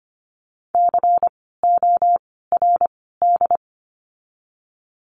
Morse code, heard as a keyed tone, about 25 wpm, 700 Hz. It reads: C O R D
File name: course / 25:10wpm / 094 Lesson 16f - Letter c - Words 25wpm